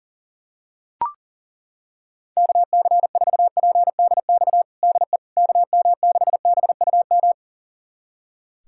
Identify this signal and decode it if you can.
Morse code, heard as a keyed tone, about 40 wpm, 700 Hz. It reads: KC4PDX DE KM6BUM
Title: USA Callsign DE Callsign 40wpm